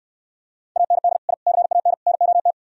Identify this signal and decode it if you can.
Morse code, heard as a keyed tone, about 68 wpm, 700 Hz. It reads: AND I CAN NOT